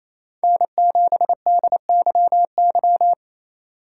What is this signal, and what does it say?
Morse code, heard as a keyed tone, about 28 wpm, 700 Hz. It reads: N7DYY
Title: USA Call Signs - 5 Characters - 2x Character Spacing - 28wpm